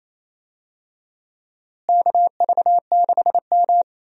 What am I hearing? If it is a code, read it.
Morse code, heard as a keyed tone, about 28 wpm, 700 Hz. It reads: KV6M
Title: USA Call Signs - 4 Characters - Repeated 3x - 28wpm